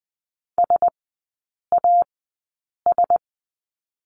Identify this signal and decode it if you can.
Morse code, heard as a keyed tone, about 20 wpm, spaced as for 8 wpm, 700 Hz. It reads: SAS
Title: 100 Lesson 17e - Letter u - Three Characters 20wpm